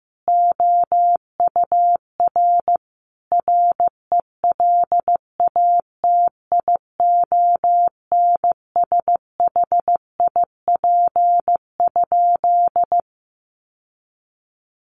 Morse code, heard as a keyed tone, about 15 wpm, 700 Hz. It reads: OUR RELATIONSHIP?
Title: Sentences from Top 1100 Words 15wpm